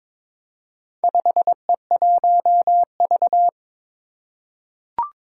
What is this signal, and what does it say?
Morse code, heard as a keyed tone, about 22 wpm, 700 Hz. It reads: 5E1V
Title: Four Challenging Letter-Number Combinations 22wpm